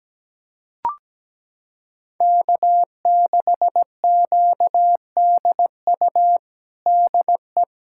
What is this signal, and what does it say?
Morse code, heard as a keyed tone, about 17 wpm, 700 Hz. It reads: K6QDU DE
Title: USA Callsign DE Callsign 17wpm